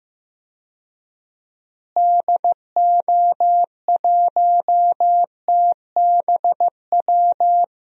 Morse code, heard as a keyed tone, about 16 wpm, 700 Hz. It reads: DO1TBW
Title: DXCC-100 Call Signs - 6 Characters - Repeated 2x - 15wpm